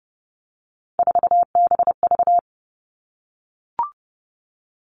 Morse code, heard as a keyed tone, about 32 wpm, 700 Hz. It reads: 4BV
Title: Three Challenging Letter-Number Combinations 30wpm